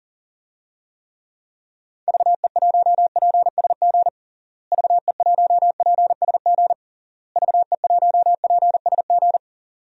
Morse code, heard as a keyed tone, about 40 wpm, 700 Hz. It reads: VE1PSG VE1PSG VE1PSG
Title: DXCC-100 Call Signs - 6 Characters - Repeated 3x - 40wpm